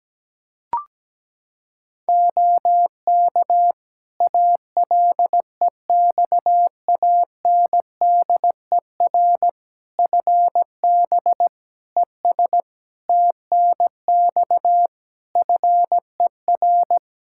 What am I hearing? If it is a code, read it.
Morse code, heard as a keyed tone, about 17 wpm, 700 Hz. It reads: OK ALEXANDER FB ES TNX FER
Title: Practice QSO - Exchange 2 - Info - 17wpm